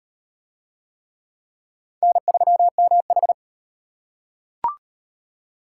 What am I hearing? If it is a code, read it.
Morse code, heard as a keyed tone, about 34 wpm, 700 Hz. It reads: N3MH